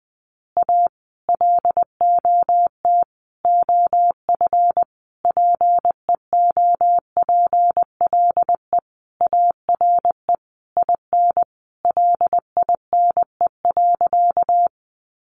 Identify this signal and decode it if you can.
Morse code, heard as a keyed tone, about 19 wpm, 700 Hz. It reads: A LOT OF PEOPLE ARE IN LINE.
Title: Additional Easy Phrases 20wpm